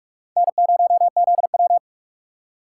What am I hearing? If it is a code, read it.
Morse code, heard as a keyed tone, about 65 wpm, spaced as for 55 wpm, 700 Hz. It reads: N0ZW
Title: Speed Racing - USA 1x2 Call Signs 45wpm